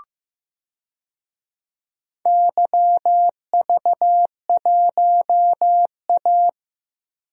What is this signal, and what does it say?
Morse code, heard as a keyed tone, about 16 wpm, 700 Hz. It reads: YV1A
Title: DXCC-100 Call Signs - 4 Characters - Repeated 3x - 15wpm